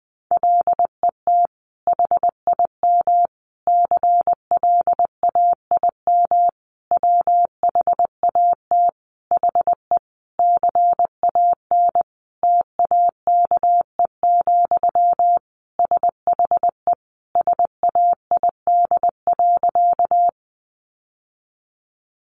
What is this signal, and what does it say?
Morse code, heard as a keyed tone, about 20 wpm, 700 Hz. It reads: LET HIM CLAIM WHAT HE CAN TAKE, SHE SAID.
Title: Sentences from Top 1500 Words 20wpm